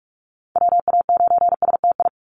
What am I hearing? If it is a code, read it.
Morse code, heard as a keyed tone, about 44 wpm, 700 Hz. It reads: WA9STI
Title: NA SOTA Activator Callsigns 45wpm